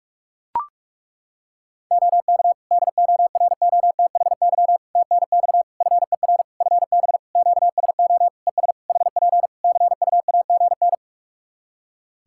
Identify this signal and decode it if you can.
Morse code, heard as a keyed tone, about 42 wpm, 700 Hz. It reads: OK DOROTHY TNX FER FB QSO ES HP CUAGN
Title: Practice QSO - Exchange 4 - Ending - 45wpm